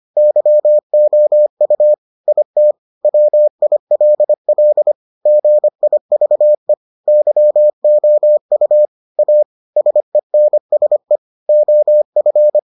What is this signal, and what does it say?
Morse code, heard as a keyed tone, about 25 wpm, 600 Hz. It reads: YOU IT WILL GIVE YOU A SENSE OF